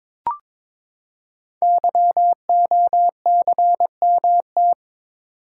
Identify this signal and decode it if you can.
Morse code, heard as a keyed tone, about 22 wpm, 700 Hz. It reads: YOCMT
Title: Code Groups - 5 Letters - 22wpm